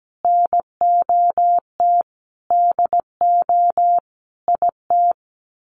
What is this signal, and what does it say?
Morse code, heard as a keyed tone, about 17 wpm, 700 Hz. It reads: NOT DO IT
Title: Sets of 5 Words Encore2 17wpm